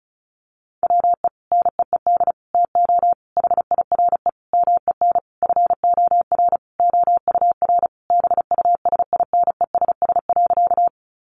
Morse code, heard as a keyed tone, about 34 wpm, 700 Hz. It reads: WE NEED TO HIRE MEN FOR OUR BUSINESS.
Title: Sentences from Top 1400 Words 35wpm